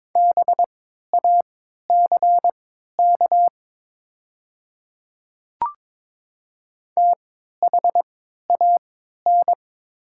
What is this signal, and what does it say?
Morse code, heard as a keyed tone, about 22 wpm, spaced as for 12 wpm, 700 Hz. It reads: BACK THAN
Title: 3x Character Spacing - Top 300 Words - 22wpm